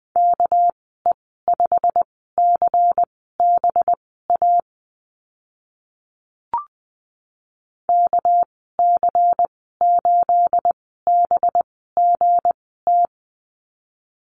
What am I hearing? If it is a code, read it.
Morse code, heard as a keyed tone, about 20 wpm, spaced as for 14 wpm, 700 Hz. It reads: KE5CBA KC8BGT
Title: 219 Lesson 35g - Number 0 - Call Signs 20wpm